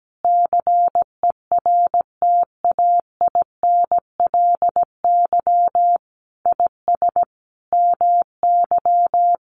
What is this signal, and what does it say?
Morse code, heard as a keyed tone, about 17 wpm, 700 Hz. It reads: CERTAINLY IS MY
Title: Sentences from Top 1200 Words 17wpm